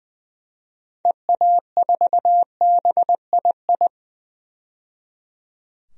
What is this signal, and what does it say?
Morse code, heard as a keyed tone, about 20 wpm, 700 Hz. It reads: EA4BII
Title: DXCC-100 Call Signs - 6 Characters - 20wpm